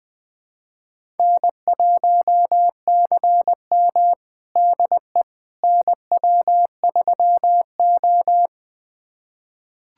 Morse code, heard as a keyed tone, about 21 wpm, 700 Hz. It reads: N1CM DE NW3O
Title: USA Callsign DE Callsign - 4 Characters - 20wpm